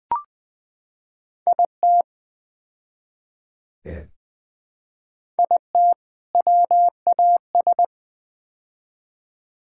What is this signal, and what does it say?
Morse code, heard as a keyed tone, about 20 wpm, 700 Hz. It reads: IT IT WAS
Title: Build A Phrase - 4-Word Phrases - 20wpm